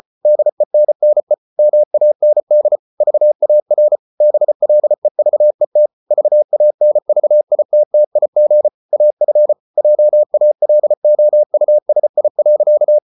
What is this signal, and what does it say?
Morse code, heard as a keyed tone, about 34 wpm, 600 Hz. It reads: DENNE MAND VAR BLEVET VANVITTIG AF JALOUSI.
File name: Ferie QRQ 34wpm